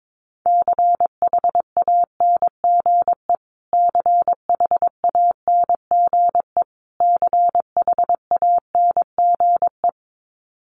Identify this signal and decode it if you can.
Morse code, heard as a keyed tone, about 22 wpm, 700 Hz. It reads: CHANGE CHANGE CHANGE
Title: Top 200 Words Repeated 22wpm